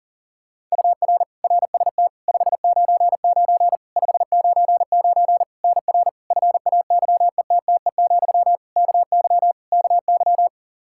Morse code, heard as a keyed tone, about 37 wpm, 700 Hz. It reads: UR RST 599 599 NR FAYETTE, KY KY
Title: Practice QSO - Exchange 1 - Beginning - 40wpm